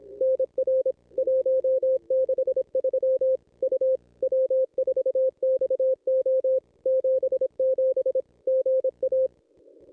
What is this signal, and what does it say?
Morse code, heard as a keyed tone, about 27 wpm, 520 Hz. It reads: NR 163 U W4XO 77 GA
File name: contests / ss-cw-k5tr-op-n5rz -1000